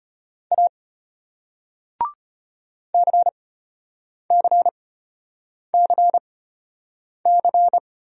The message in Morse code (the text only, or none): A CCCC